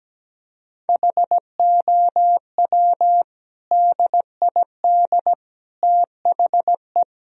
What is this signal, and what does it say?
Morse code, heard as a keyed tone, about 17 wpm, 700 Hz. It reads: HOW DID THE
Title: Sentences from Top 1000 Words - Review Part 4 - 17wpm